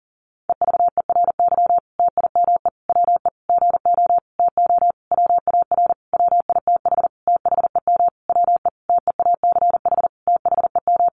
Morse code, heard as a keyed tone, about 40 wpm, 700 Hz. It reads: EVERY TIME WE GO TO WAR WITH THEM WE TEACH THEM
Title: Sentences from Top 1300 Words 40wpm